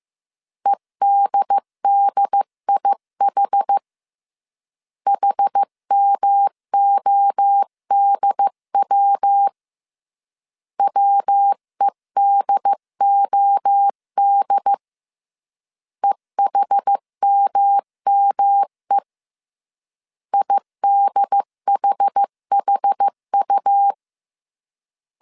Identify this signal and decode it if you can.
Morse code, heard as a keyed tone, about 15 wpm, 800 Hz. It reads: EDDIH HMODW WEDOD EHMME IDHHU
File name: CW / DET / 1 3 1500C S A 14wpm 2447